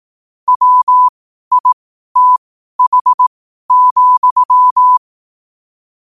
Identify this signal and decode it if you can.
Morse code, heard as a keyed tone, about 18 wpm, spaced as for 12 wpm, 1000 Hz. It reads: WITH,
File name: qotdfiles / 12wpm